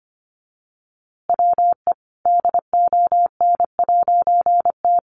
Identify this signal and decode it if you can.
Morse code, heard as a keyed tone, about 25 wpm, 700 Hz. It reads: WE DON'T
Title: A Sound Of Thunder 25wpm